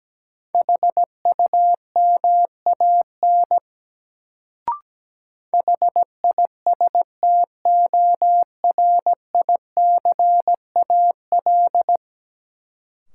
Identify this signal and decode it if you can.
Morse code, heard as a keyed tone, about 17 wpm, 700 Hz. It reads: HUMAN HISTORICAL
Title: Sentences from Top 600 Words - New Word Drill - 17wpm